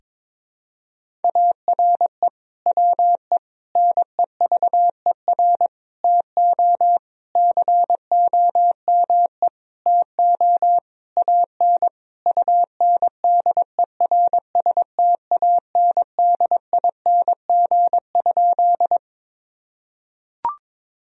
Morse code, heard as a keyed tone, about 22 wpm, 700 Hz. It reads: ARE WE NEVER TO COME TO AN UNDERSTANDING?
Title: Sentences from Top 1000 Words - Review Part 3 - 22wpm